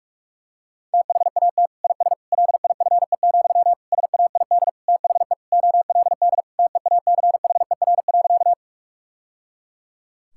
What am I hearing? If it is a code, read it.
Morse code, heard as a keyed tone, about 46 wpm, 700 Hz. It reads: THAT IS LIFE, SAID THE OLD TEACHER.
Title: Sentences from Top 1000 Words - Review Part 6 - 45wpm